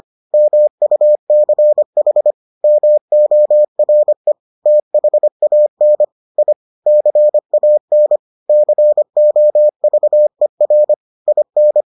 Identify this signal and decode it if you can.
Morse code, heard as a keyed tone, about 25 wpm, 600 Hz. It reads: MUCH MORE THAN I CAN COVER IN